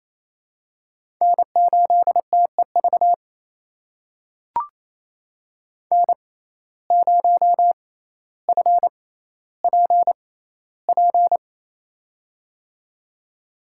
Morse code, heard as a keyed tone, about 28 wpm, spaced as for 14 wpm, 700 Hz. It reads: N8TEV N0FPP
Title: USA Call Signs - 6x Character Spacing - 28wpm